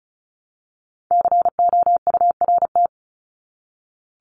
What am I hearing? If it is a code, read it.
Morse code, heard as a keyed tone, about 34 wpm, 700 Hz. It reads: COURT
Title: Top 1501-1750 Words 35wpm